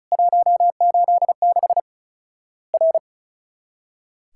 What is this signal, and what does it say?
Morse code, heard as a keyed tone, about 36 wpm, 700 Hz. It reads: 186 R
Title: Worked All Europe DX Contest - QTC Exchange - 35wpm